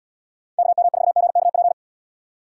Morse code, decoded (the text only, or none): KM4GKP